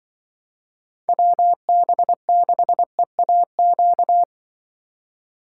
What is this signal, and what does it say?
Morse code, heard as a keyed tone, about 26 wpm, 700 Hz. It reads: WB6EAQ